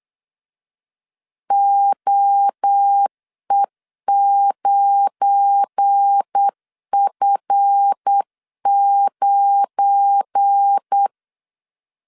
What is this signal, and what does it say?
Morse code, heard as a keyed tone, about 8 wpm, 800 Hz. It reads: OE9F9
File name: LEZIONI / 7 GF9Z / 7 GF9 8wpm 3545